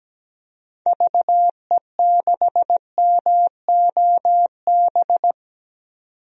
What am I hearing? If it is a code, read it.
Morse code, heard as a keyed tone, about 17 wpm, 700 Hz. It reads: VE6MOB
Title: DXCC-100 Call Signs - 6 Characters - 17wpm